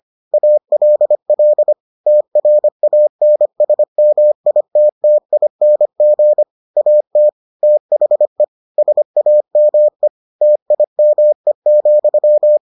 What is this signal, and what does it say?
Morse code, heard as a keyed tone, about 25 wpm, 600 Hz. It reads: ALL TRANSMITTING AT THE SAME TIME,